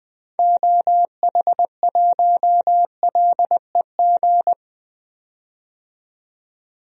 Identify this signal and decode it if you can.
Morse code, heard as a keyed tone, about 20 wpm, 700 Hz. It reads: OH1LEG
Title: DXCC-100 Call Signs - 6 Characters - 20wpm